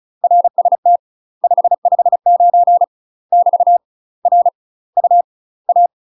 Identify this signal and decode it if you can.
Morse code, heard as a keyed tone, about 34 wpm, 700 Hz. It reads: RST 559 <BT> R U A
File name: UK MASTER QSOs TRAINING / 35WPM / UK MASTER QSOs 0034